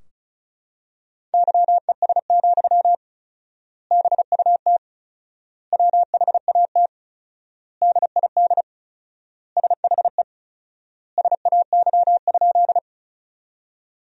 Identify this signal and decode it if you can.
Morse code, heard as a keyed tone, about 36 wpm, 700 Hz. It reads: YES, BUT WHAT DID SHE SAY?
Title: 4x Word-Spacing - Sentences from Top 500 Words - 35wpm